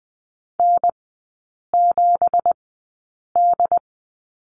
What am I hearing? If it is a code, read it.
Morse code, heard as a keyed tone, about 20 wpm, spaced as for 8 wpm, 700 Hz. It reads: N7D
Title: 166 Lesson 27e - Number 7 - Three Characters 20wpm